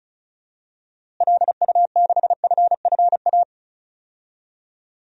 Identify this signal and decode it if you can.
Morse code, heard as a keyed tone, about 34 wpm, 700 Hz. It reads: LU6FFA
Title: DXCC-100 Call Signs - 6 Characters - Repeated 3x - 35wpm